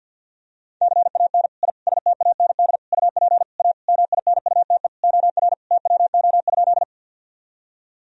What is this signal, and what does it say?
Morse code, heard as a keyed tone, about 45 wpm, 700 Hz. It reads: CAN I STAND UP A MINUTE OR TWO?
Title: Sentences from Top 1000 Words - Review Part 1 - 50wpm